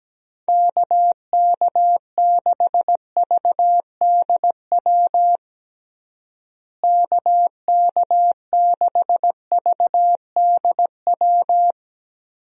Morse code, heard as a keyed tone, about 17 wpm, 700 Hz. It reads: KK6VDW KK6VDW
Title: USA Call Signs - 6 Characters - Repeated 3x - 17wpm